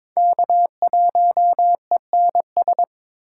Morse code, heard as a keyed tone, about 26 wpm, 700 Hz. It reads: K1ENS
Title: Speed Racing - USA 1x3 Call Signs 22wpm